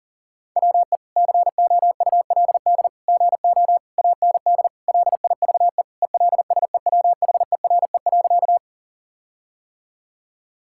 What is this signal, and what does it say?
Morse code, heard as a keyed tone, about 40 wpm, 700 Hz. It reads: WE COULD GO AND LIVE ELSEWHERE.
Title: Sentences from Top 1600 Words 40wpm